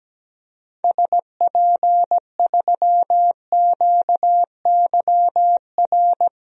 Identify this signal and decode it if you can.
Morse code, heard as a keyed tone, about 17 wpm, 700 Hz. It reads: SP3QYR
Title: DXCC-100 Call Signs - 6 Characters - Repeated 3x - 17wpm